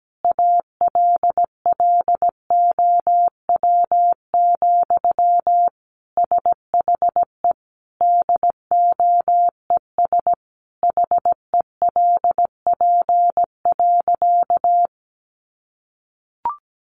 Morse code, heard as a keyed tone, about 17 wpm, 700 Hz. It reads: ALLOW, SHE DOES HELP.
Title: Sentences from Top 700 Words 17wpm